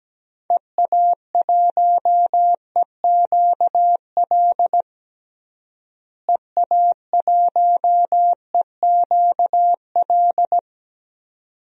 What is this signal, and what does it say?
Morse code, heard as a keyed tone, about 17 wpm, 700 Hz. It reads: EA1EQL EA1EQL
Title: DXCC-100 Call Signs - 6 Characters - Repeated 2x - 17wpm